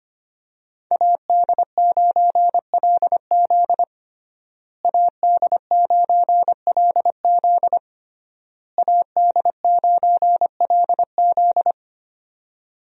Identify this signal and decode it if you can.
Morse code, heard as a keyed tone, about 25 wpm, 700 Hz. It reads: AD9LZ AD9LZ AD9LZ
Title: USA Call Signs - 5 Characters - Repeated 3x - 25wpm